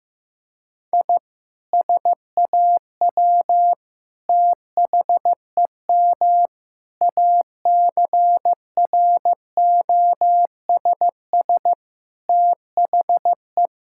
Morse code, heard as a keyed tone, about 15 wpm, 700 Hz. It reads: I SAW THEM ACROSS THE
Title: Sentences from Top 1000 Words - Review Part 5 - 15wpm